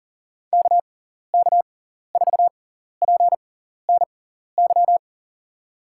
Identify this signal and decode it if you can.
Morse code, heard as a keyed tone, about 40 wpm, spaced as for 14 wpm, 700 Hz. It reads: KK4PNY
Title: USA Call Signs - 6 Characters - 6x Character Spacing - 40wpm